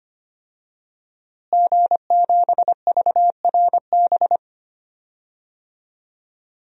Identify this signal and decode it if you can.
Morse code, heard as a keyed tone, about 25 wpm, 700 Hz. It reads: G7VRB
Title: DXCC-100 Call Signs - 5 Characters - Repeated 3x - 25wpm